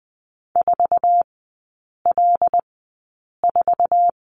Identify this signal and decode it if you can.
Morse code, heard as a keyed tone, about 20 wpm, spaced as for 8 wpm, 700 Hz. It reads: 4L4
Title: 074 Lesson 13e - Letter L - Three Characters 20wpm